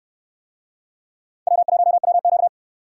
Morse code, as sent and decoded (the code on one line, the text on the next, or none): .-- .---- .-- --.-
W1WQ